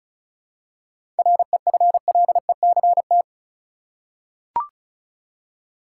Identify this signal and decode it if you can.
Morse code, heard as a keyed tone, about 34 wpm, 700 Hz. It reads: REFLECT
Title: Seven Letter Words - 35wpm